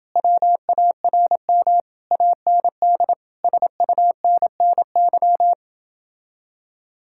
Morse code, heard as a keyed tone, about 31 wpm, 700 Hz. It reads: WARM AND SUNNY